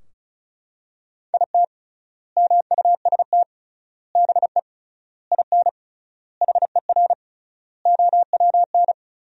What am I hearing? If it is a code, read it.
Morse code, heard as a keyed tone, about 36 wpm, 700 Hz. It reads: IT MUST BE IN HER OWN
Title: 3x Word-Spacing - Sentences from Top 300 Words - 35wpm